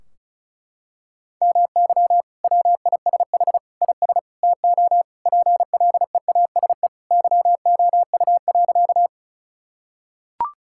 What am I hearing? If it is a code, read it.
Morse code, heard as a keyed tone, about 36 wpm, 700 Hz. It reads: MY WISH IS TO PLEASE YOU.
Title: Sentences from Top 1600 Words 35wpm